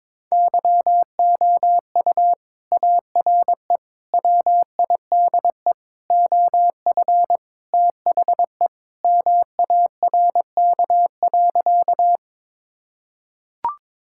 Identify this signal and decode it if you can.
Morse code, heard as a keyed tone, about 22 wpm, 700 Hz. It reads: YOU ARE WIDE OF THE MARK.
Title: Sentences from Top 1000 Words - Review Part 6 - 22wpm